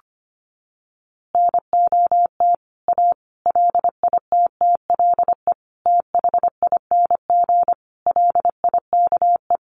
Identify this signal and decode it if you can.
Morse code, heard as a keyed tone, about 25 wpm, 700 Hz. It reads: NOT A LITTLE THING LIKE
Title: A Sound Of Thunder 25wpm